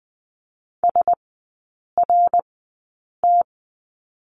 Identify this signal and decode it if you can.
Morse code, heard as a keyed tone, about 20 wpm, spaced as for 8 wpm, 700 Hz. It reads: SRT
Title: 074 Lesson 13e - Letter L - Three Characters 20wpm